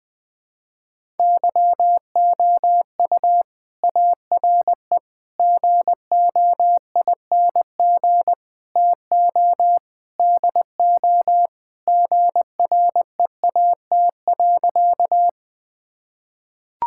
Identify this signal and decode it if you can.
Morse code, heard as a keyed tone, about 20 wpm, 700 Hz. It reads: YOU ARE GOING TO DO GREAT.
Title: Sentences with 4-Word Phrases - 20wpm